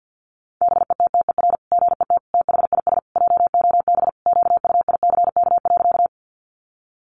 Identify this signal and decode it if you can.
Morse code, heard as a keyed tone, about 48 wpm, 700 Hz. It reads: BETTER GET THIS JOB QUICK.
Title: Sentences from Top 600 Words 50wpm